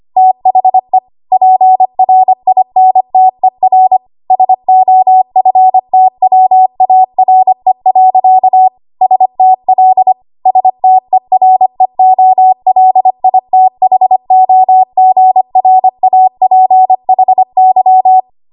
Morse code, heard as a keyed tone, about 25 wpm, 750 Hz. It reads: THE PRINTER SOFTWARE. STL STEREOLITHOGRAPHY
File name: Archive / 25 WPM / 25WPM